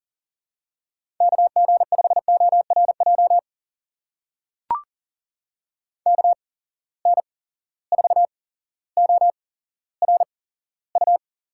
Morse code, heard as a keyed tone, about 40 wpm, spaced as for 17 wpm, 700 Hz. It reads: KG5ORJ KN4ORU
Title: USA Call Signs - 6 Characters - 8x Character Spacing - 40wpm